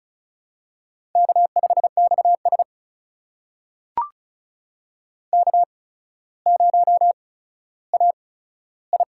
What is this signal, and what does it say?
Morse code, heard as a keyed tone, about 36 wpm, spaced as for 15 wpm, 700 Hz. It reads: K5XS K0AI
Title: USA Call Signs - 4 Characters - 8x Character Spacing - 35wpm